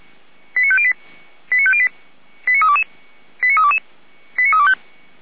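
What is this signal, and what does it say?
Five-tone selective call (ZVEI), heard as quick runs of five tones, tones of about 70 ms. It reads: 89508 89508 89322 89322 89325
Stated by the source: ZVEI five-tone call received by SDR, from Feuerwehr Orlamünde audio server.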